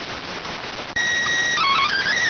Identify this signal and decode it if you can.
SSTV (slow-scan television), PD 290.